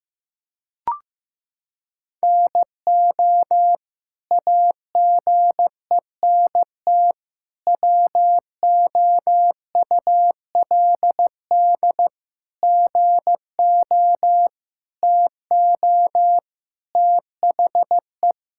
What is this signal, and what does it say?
Morse code, heard as a keyed tone, about 15 wpm, 700 Hz. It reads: NO AGENT WOULD GO TO THE
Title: Sentences from Top 1600 Words 15wpm